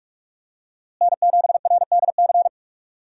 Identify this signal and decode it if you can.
Morse code, heard as a keyed tone, about 46 wpm, 700 Hz. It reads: N7RDC
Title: USA Call Signs - 5 Characters - Repeated 2x - 45wpm